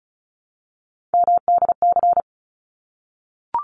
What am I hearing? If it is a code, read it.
Morse code, heard as a keyed tone, about 34 wpm, 700 Hz. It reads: MDC
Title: ARRL Sections 35wpm